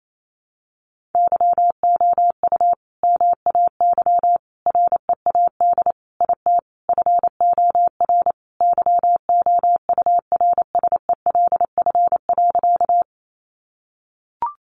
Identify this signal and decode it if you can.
Morse code, heard as a keyed tone, about 28 wpm, 700 Hz. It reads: YOU MAY READ IT FOR YOURSELF.
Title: Sentences from Top 600 Words 28wpm